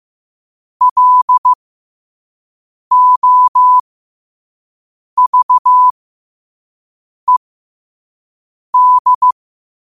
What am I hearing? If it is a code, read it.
Morse code, heard as a keyed tone, about 15 wpm, spaced as for 5 wpm, 1000 Hz. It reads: LOVED